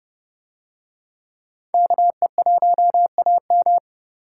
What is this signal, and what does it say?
Morse code, heard as a keyed tone, about 29 wpm, 700 Hz. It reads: KE1AM